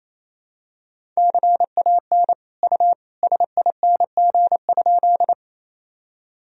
Morse code, heard as a keyed tone, about 28 wpm, 700 Hz. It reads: CAN U SING?
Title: Sentences from Top 400 Words and Abbreviations 28wpm